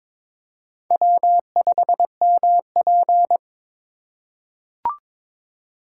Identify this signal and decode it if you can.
Morse code, heard as a keyed tone, about 21 wpm, 700 Hz. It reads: W5MP
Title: USA Call Signs - 4 Characters - 2x Character Spacing - 22wpm